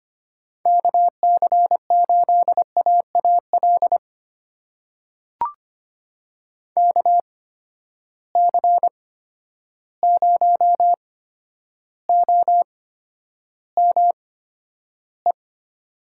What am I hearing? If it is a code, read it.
Morse code, heard as a keyed tone, about 25 wpm, spaced as for 11 wpm, 700 Hz. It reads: KC8AAL KC0OME
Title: USA Call Signs - 6 Characters - 8x Character Spacing - 25wpm